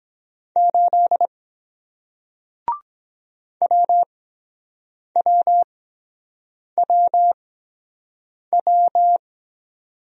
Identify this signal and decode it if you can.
Morse code, heard as a keyed tone, about 23 wpm, 700 Hz. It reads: 8 W W W W